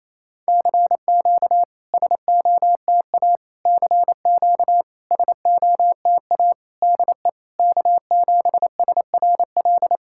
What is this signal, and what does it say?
Morse code, heard as a keyed tone, about 28 wpm, 700 Hz. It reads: CQ SOTA CQ SOTA DE K7SRL
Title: SOTA QSO Practice 28wpm